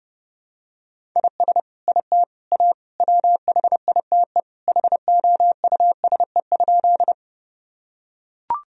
Morse code, heard as a keyed tone, about 30 wpm, 700 Hz. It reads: IS IT A WHITE HOUSE?
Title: Sentences from Top 1000 Words - Review Part 1 - 30wpm